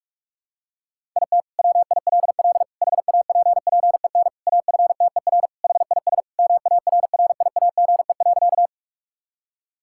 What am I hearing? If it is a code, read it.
Morse code, heard as a keyed tone, about 41 wpm, 700 Hz. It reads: IT WILL HAPPEN AFTER HIS MARRIAGE.
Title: Sentences from Top 1000 Words - Review Part 5 - 45wpm